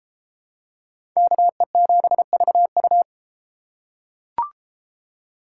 Morse code, heard as a keyed tone, about 32 wpm, 700 Hz. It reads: KE7VU